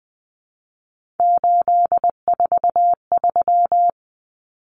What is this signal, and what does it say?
Morse code, heard as a keyed tone, about 20 wpm, 700 Hz. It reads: 843
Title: Common Cut Numbers - 3-Digit 20wpm